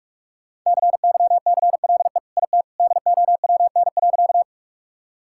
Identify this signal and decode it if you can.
Morse code, heard as a keyed tone, about 46 wpm, 700 Hz. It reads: CYCLE IT DOWN.
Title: Sentences from Top 1000 Words - Review Part 6 - 45wpm